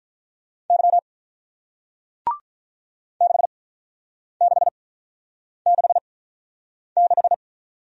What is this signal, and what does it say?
Morse code, heard as a keyed tone, about 48 wpm, spaced as for 9 wpm, 700 Hz. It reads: Y 6666